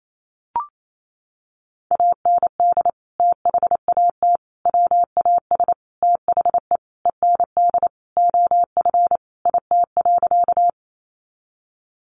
Morse code, heard as a keyed tone, about 28 wpm, 700 Hz. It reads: AND THAT WAS THE END OF IT.
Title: Sentences from Top 200 Words 28wpm